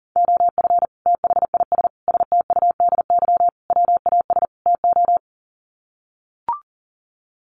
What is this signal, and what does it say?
Morse code, heard as a keyed tone, about 37 wpm, 700 Hz. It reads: OF THIS STUDY WAS TO
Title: Sets of 5 Words 40wpm